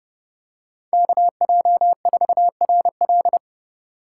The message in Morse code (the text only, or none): KJ4RL